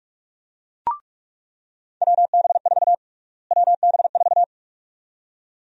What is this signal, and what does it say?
Morse code, heard as a keyed tone, about 41 wpm, 700 Hz. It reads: WB4 WB4
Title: USA Call Sign Prefixes - Repeated 2x - 45wpm